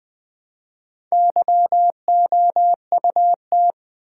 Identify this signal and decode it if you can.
Morse code, heard as a keyed tone, about 20 wpm, 700 Hz. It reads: YOUT
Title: Build A Word - 1 Letter At A Time - 5-Letter Words - 20wpm